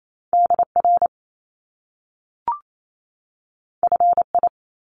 Morse code, heard as a keyed tone, about 28 wpm, 700 Hz. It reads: DR FI
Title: Top Two Letter Combinations in Words 28wpm